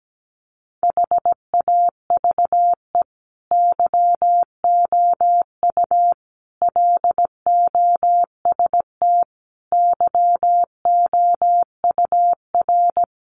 Morse code, heard as a keyed tone, about 17 wpm, 700 Hz. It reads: HAVE YOU LOST YOUR
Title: Sentences from Top 1400 Words 17wpm